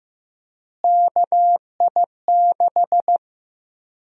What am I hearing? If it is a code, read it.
Morse code, heard as a keyed tone, about 16 wpm, 700 Hz. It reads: KI6
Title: USA Call Sign Prefixes - Repeated 2x - 15wpm